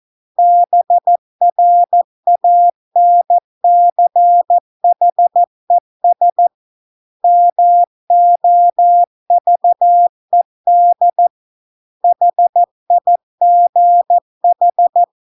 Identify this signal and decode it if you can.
Morse code, heard as a keyed tone, about 14 wpm, 700 Hz. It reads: BRANCHES MOVED HIGH